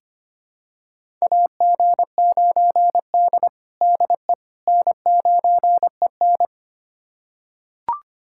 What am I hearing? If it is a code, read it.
Morse code, heard as a keyed tone, about 25 wpm, 700 Hz. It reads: AG9D DE N9EN
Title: USA Callsign DE Callsign - 4 Characters - 25wpm